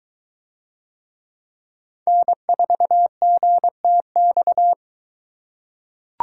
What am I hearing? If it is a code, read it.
Morse code, heard as a keyed tone, about 22 wpm, 700 Hz. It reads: N4GTX